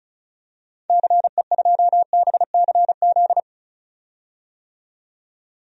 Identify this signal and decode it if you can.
Morse code, heard as a keyed tone, about 34 wpm, 700 Hz. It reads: CE2BCZ
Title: DXCC-100 Call Signs - 6 Characters - 35wpm